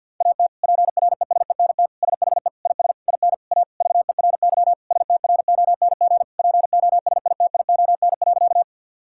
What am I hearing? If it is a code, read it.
Morse code, heard as a keyed tone, about 50 wpm, 690 Hz. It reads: AT PRESENT SHE IS IN A VERY STRONG POSITION.
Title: Sentences from Top 1000 Words - Review Part 5 - 50wpm